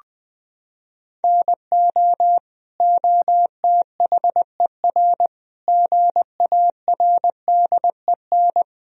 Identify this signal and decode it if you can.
Morse code, heard as a keyed tone, about 20 wpm, 700 Hz. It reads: NO OTHER GARDEN
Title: Sentences from Top 1000 Words - Review Part 4 - 20wpm